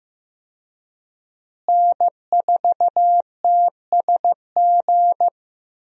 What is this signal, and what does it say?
Morse code, heard as a keyed tone, about 15 wpm, 700 Hz. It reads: N4TSG
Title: USA Call Signs - 5 Characters - Repeated 3x - 15wpm